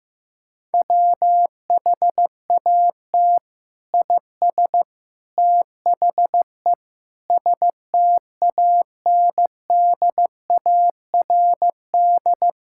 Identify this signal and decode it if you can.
Morse code, heard as a keyed tone, about 15 wpm, 700 Hz. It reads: WHAT IS THE STANDARD
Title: Sentences from Top 1000 Words - Review Part 6 - 15wpm